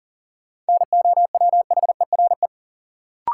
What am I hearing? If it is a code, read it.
Morse code, heard as a keyed tone, about 40 wpm, 700 Hz. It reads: NOWHERE
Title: Sets of 5 Words Encore - New Word Drill - 40wpm